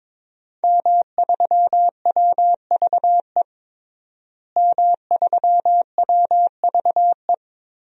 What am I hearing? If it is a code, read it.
Morse code, heard as a keyed tone, about 21 wpm, 700 Hz. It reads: M3WVE M3WVE
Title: DXCC-100 Call Signs - 5 Characters - Repeated 3x - 22wpm